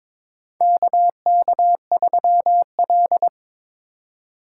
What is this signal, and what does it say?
Morse code, heard as a keyed tone, about 22 wpm, 700 Hz. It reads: KK3L
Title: USA Call Signs - 4 Characters - 2x Character Spacing - 22wpm